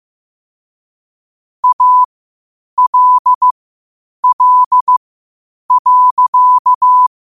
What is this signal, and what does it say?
Morse code, heard as a keyed tone, about 15 wpm, spaced as for 8 wpm, 1000 Hz. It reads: ALL.
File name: qotdfiles / 8wpm